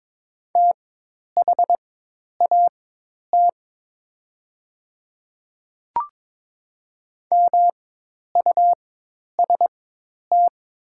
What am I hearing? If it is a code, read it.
Morse code, heard as a keyed tone, about 22 wpm, spaced as for 10 wpm, 700 Hz. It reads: THAT MUST
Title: 4x Character Spacing - Top 200 Words - 22wpm